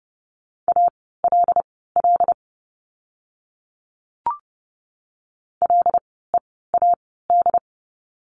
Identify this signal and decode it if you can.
Morse code, heard as a keyed tone, about 30 wpm, spaced as for 17 wpm, 700 Hz. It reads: ALL LEAD
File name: Lesson 17f - Letter u - Words 30wpm